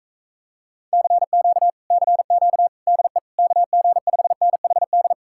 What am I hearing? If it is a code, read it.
Morse code, heard as a keyed tone, about 37 wpm, 700 Hz. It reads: CQ CQ DE KG5NHD